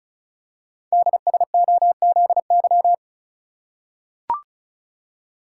Morse code, heard as a keyed tone, about 36 wpm, 700 Hz. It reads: DSOZY
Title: Code Groups - 5 Letters - 35wpm